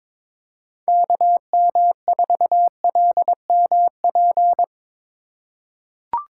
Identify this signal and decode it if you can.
Morse code, heard as a keyed tone, about 22 wpm, 700 Hz. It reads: KM4LMP
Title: USA Call Signs - 6 Characters - 2x Character Spacing - 22wpm